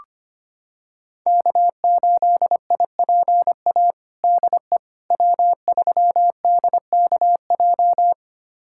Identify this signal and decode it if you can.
Morse code, heard as a keyed tone, about 25 wpm, 700 Hz. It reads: K8IPA DE W3DKJ
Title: USA Callsign DE Callsign - 5 Characters - 25wpm